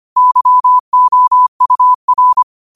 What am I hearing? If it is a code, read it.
Morse code, heard as a keyed tone, about 25 wpm, 1000 Hz. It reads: YOUR
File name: qotdfiles / 25wpm